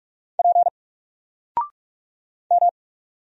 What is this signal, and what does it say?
Morse code, heard as a keyed tone, about 44 wpm, 700 Hz. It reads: P M
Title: Single Letter-Number 45wpm